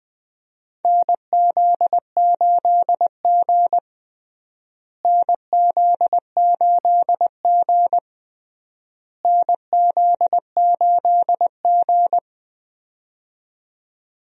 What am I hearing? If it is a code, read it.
Morse code, heard as a keyed tone, about 20 wpm, 700 Hz. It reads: NZ8G NZ8G NZ8G
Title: USA Call Signs - 4 Characters - Repeated 3x - 20wpm